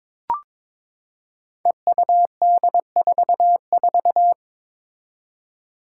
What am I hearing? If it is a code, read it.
Morse code, heard as a keyed tone, about 21 wpm, 700 Hz. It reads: EUD44
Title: Five Challenging Letter-Number Combinations 22wpm